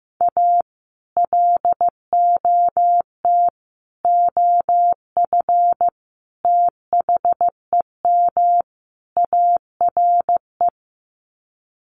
Morse code, heard as a keyed tone, about 15 wpm, 700 Hz. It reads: A LOT OF THEM ARE
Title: Sets of 5 Words Encore2 15wpm